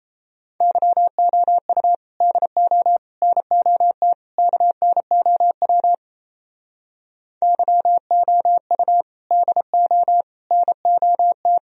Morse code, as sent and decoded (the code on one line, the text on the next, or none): -.-- --- ..- / -.. --- / -. --- - / -.- -. --- .-- / -.-- --- ..- / -.. --- / -. --- -
YOU DO NOT KNOW YOU DO NOT